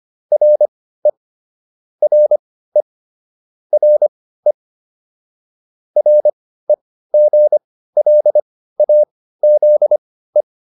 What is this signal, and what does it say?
Morse code, heard as a keyed tone, about 25 wpm, spaced as for 15 wpm, 600 Hz. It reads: RE RE RE REGLAZE